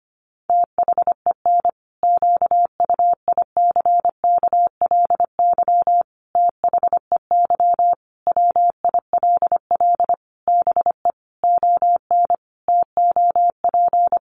Morse code, heard as a keyed tone, about 25 wpm, 700 Hz. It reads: THEN QUICKLY THEY WILL BE ON TOP
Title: Sentences from Top 700 Words 25wpm